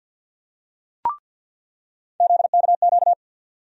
Morse code, heard as a keyed tone, about 48 wpm, 700 Hz. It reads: ZKQ
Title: USA Call Sign Postfixes - 3 Letters 50wpm